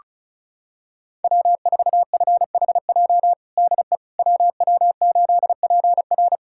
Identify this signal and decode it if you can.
Morse code, heard as a keyed tone, about 34 wpm, 700 Hz. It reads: W4FHJ DE WW8PR
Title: USA Callsign DE Callsign - 5 Characters - 35wpm